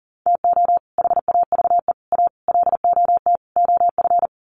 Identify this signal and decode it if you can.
Morse code, heard as a keyed tone, about 40 wpm, 700 Hz. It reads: TO HAVE A LOT OF
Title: Sets of 5 Words Encore 40wpm